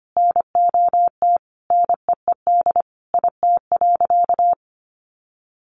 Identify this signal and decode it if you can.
Morse code, heard as a keyed tone, about 25 wpm, 700 Hz. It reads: NOT NEED IT.
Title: Sentences from Top 1400 Words 25wpm